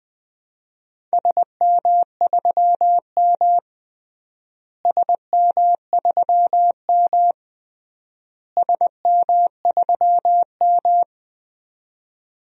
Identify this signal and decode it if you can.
Morse code, heard as a keyed tone, about 20 wpm, 700 Hz. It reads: SM3M SM3M SM3M
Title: DXCC-100 Call Signs - 4 Characters - Repeated 3x - 20wpm